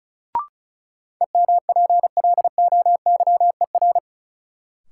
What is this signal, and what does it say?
Morse code, heard as a keyed tone, about 36 wpm, 700 Hz. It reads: EMPLOYER
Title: Sentences from Top 1600 Words - New Word Drill - 35wpm